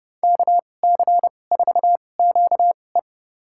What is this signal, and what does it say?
Morse code, heard as a keyed tone, about 30 wpm, spaced as for 22 wpm, 700 Hz. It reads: KC4QE
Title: USA Call Signs - 5 Characters - 2x Character Spacing - 30wpm